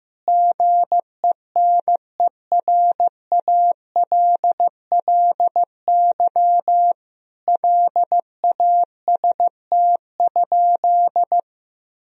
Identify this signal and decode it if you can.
Morse code, heard as a keyed tone, about 16 wpm, 700 Hz. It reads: GENERALLY LAST?
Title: Sentences from Top 1000 Words - Review Part 4 - 15wpm